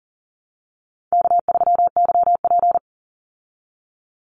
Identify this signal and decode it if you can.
Morse code, heard as a keyed tone, about 38 wpm, 700 Hz. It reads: K3YP
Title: Call Signs Repeated 2x - 40wpm